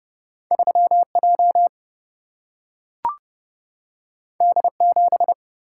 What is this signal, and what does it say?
Morse code, heard as a keyed tone, about 30 wpm, 700 Hz. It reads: 3J D7
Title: Single-Letter Single-Number 30wpm